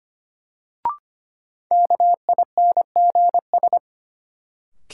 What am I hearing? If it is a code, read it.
Morse code, heard as a keyed tone, about 25 wpm, 700 Hz. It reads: KINGS
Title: Sentences from Top 1000 Words - Review Part 6 - New Word Drill - 25wpm